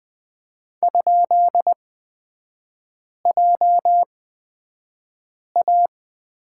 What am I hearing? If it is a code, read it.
Morse code, heard as a keyed tone, about 20 wpm, spaced as for 5 wpm, 700 Hz. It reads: ?JA
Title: 224 Lesson 36e - BT - Three Characters 20wpm